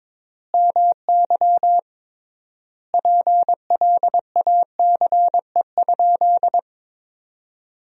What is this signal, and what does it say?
Morse code, heard as a keyed tone, about 22 wpm, 700 Hz. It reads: MY PLACE?
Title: 3x Word-Spacing - Sentences from Top 200 Words - 22wpm